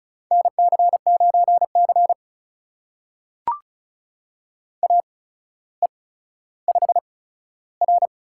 Morse code, heard as a keyed tone, about 36 wpm, spaced as for 15 wpm, 700 Hz. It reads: NC9C AE5R
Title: USA Call Signs - 4 Characters - 8x Character Spacing - 35wpm